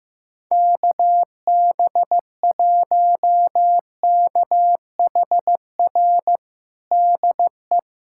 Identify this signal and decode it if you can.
Morse code, heard as a keyed tone, about 15 wpm, 700 Hz. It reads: KB1KHR DE
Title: USA Callsign DE Callsign - 6 Characters - 15wpm